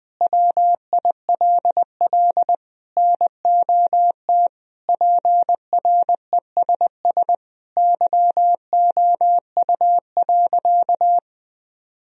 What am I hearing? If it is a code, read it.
Morse code, heard as a keyed tone, about 20 wpm, 700 Hz. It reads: WILL NOT PRESS YOU.
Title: Sentences from Top 1000 Words - Review Part 1 - 20wpm